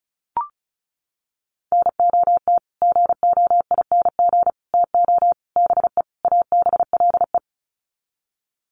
Morse code, heard as a keyed tone, about 34 wpm, 700 Hz. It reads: NOT GOING TO BE ABLE
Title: Sets of 5 Words 35wpm